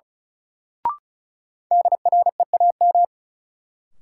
Morse code, heard as a keyed tone, about 34 wpm, 700 Hz. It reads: DREAM
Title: Sentences from Top 1500 Words - New Word Drill Alt - 35wpm